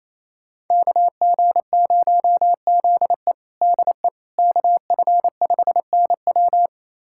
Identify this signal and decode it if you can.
Morse code, heard as a keyed tone, about 27 wpm, 700 Hz. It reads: KG0ZE DE KF5NW
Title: USA Callsign DE Callsign - 5 Characters - 28wpm